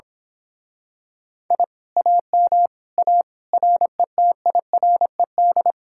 Morse code, heard as a keyed tone, about 26 wpm, 700 Hz. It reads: I AM A RETIRED
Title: Speed-Racing Common QSO Phrases - 22wpm